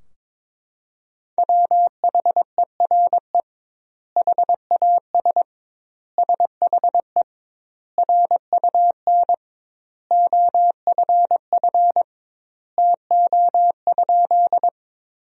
Morse code, heard as a keyed tone, about 22 wpm, 700 Hz. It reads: WHERE HAS SHE RUN OFF TO?
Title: 2x Word-Spacing - Sentences from Top 300 Words - 22wpm